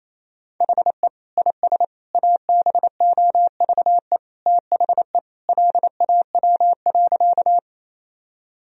Morse code, heard as a keyed tone, about 28 wpm, 700 Hz. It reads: HE IS ABOVE THE LAW.
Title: Sentences from Top 1000 Words - Review Part 3 - 28wpm